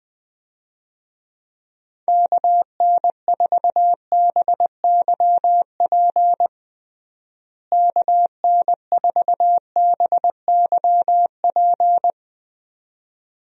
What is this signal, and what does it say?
Morse code, heard as a keyed tone, about 20 wpm, 700 Hz. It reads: KN4BYP KN4BYP
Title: USA Call Signs - 6 Characters - Repeated 2x - 20wpm